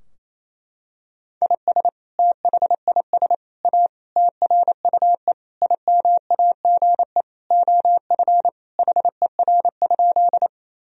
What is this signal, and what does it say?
Morse code, heard as a keyed tone, about 28 wpm, 700 Hz. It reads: IS THIS A TRUE IMAGE OF HER?
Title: Sentences from Top 1000 Words - Review Part 4 - 28wpm